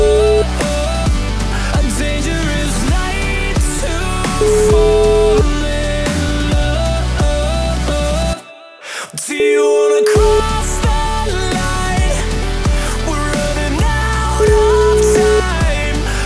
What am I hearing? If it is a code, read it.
Ringback tone (Europe, 425 Hz), with noise.